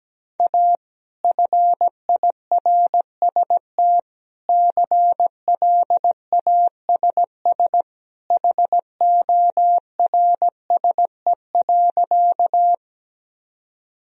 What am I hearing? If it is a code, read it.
Morse code, heard as a keyed tone, about 17 wpm, 700 Hz. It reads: A FIRST CLASS HORSE.
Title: Sentences from Top 1000 Words - Review Part 2 - 17wpm